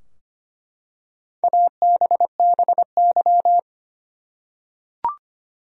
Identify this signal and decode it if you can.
Morse code, heard as a keyed tone, about 25 wpm, 700 Hz. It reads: ABBY
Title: USA Common Names - Max 5 Letters - 25wpm